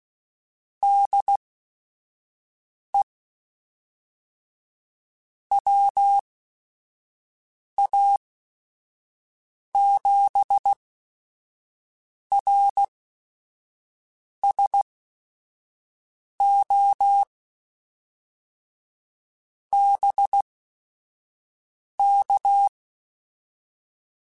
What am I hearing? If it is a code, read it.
Morse code, heard as a keyed tone, about 16 wpm, spaced as for 5 wpm, 800 Hz. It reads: DE WA7RSO BK
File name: QSOs / QSOs-mp3s / WA7QOX 1@5